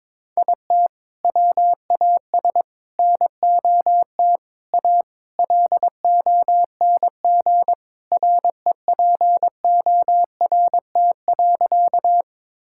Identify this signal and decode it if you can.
Morse code, heard as a keyed tone, about 22 wpm, 700 Hz. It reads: IT WAS NOT A LONG REPORT.
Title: Sentences from Top 1000 Words - Review Part 6 - 22wpm